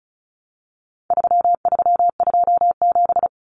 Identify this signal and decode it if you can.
Morse code, heard as a keyed tone, about 34 wpm, 700 Hz. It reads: 3327
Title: Common Cut Numbers - 4-Digit 35wpm